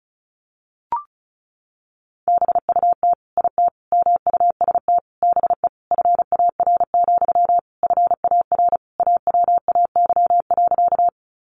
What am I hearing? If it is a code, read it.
Morse code, heard as a keyed tone, about 34 wpm, 700 Hz. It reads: BUT IT MUST BE FAR, FAR AWAY.
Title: Sentences from Top 600 Words 35wpm